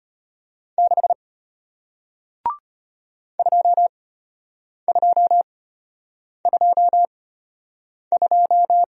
Morse code, heard as a keyed tone, about 33 wpm, spaced as for 8 wpm, 700 Hz. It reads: 6 2222